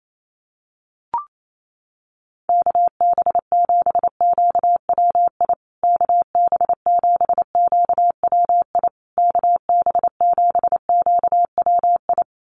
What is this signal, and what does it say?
Morse code, heard as a keyed tone, about 28 wpm, 690 Hz. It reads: KB7QWI KB7QWI KB7QWI
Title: USA Call Signs Repeated 3x 28wpm